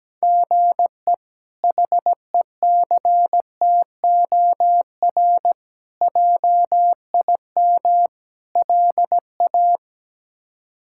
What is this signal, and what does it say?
Morse code, heard as a keyed tone, about 17 wpm, 700 Hz. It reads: GE HECTOR JIM LA
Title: K1USN SST QSO Practice - 17wpm